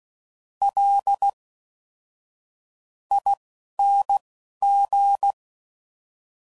Morse code, heard as a keyed tone, about 16 wpm, spaced as for 7 wpm, 800 Hz. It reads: LING